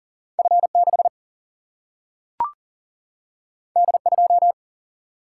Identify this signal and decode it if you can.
Morse code, heard as a keyed tone, about 40 wpm, 700 Hz. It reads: F6 D2
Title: Single-Letter Single-Number 40wpm